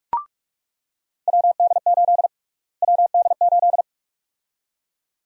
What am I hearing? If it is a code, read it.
Morse code, heard as a keyed tone, about 47 wpm, 700 Hz. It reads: WD8 WD8
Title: USA Call Sign Prefixes - Repeated 2x - 45wpm